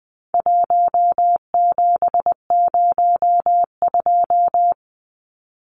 Morse code, heard as a keyed tone, about 20 wpm, 700 Hz. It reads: 1702
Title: Numbers - 4-Digit 20wpm